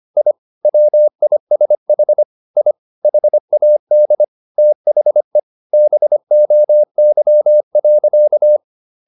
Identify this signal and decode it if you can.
Morse code, heard as a keyed tone, about 25 wpm, 600 Hz. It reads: I WISH I HAD THE BOY.
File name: the old man and the sea part 1 25wpm